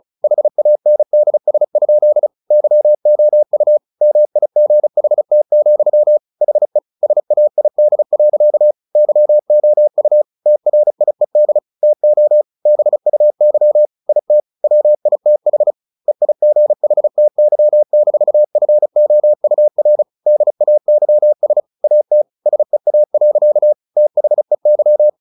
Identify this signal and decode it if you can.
Morse code, heard as a keyed tone, about 34 wpm, 600 Hz. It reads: HANDS? YOU MIGHT, HE SAID. YOU TRIED TO BUY IT WITH EIGHTY-FOUR DAYS AT SEA. THEY
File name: the old man and the sea 2 35wpm